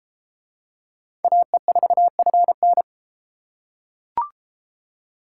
Morse code, heard as a keyed tone, about 32 wpm, 700 Hz. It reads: AE4FN